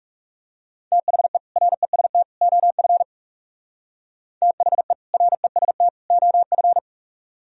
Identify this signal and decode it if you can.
Morse code, heard as a keyed tone, about 37 wpm, 700 Hz. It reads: THE REST OF THE REST OF
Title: Speed-Racing - Sets of 3 Words 30wpm